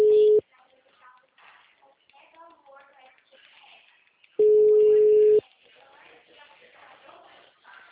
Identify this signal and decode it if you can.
Ringback tone (Europe, 425 Hz), with speech.